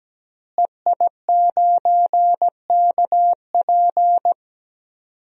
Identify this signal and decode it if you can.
Morse code, heard as a keyed tone, about 17 wpm, 700 Hz. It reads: EI9KP
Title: DXCC-100 Call Signs - 5 Characters - Repeated 2x - 17wpm